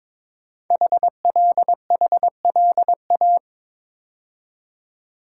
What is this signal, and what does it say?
Morse code, heard as a keyed tone, about 22 wpm, 700 Hz. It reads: HLHLA
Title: Code Groups - 5 Letters - 22wpm